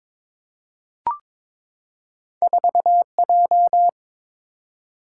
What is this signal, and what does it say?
Morse code, heard as a keyed tone, about 22 wpm, 700 Hz. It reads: 4J
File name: practice / 22wpm / Two Letter-Number Combinations - 22wpm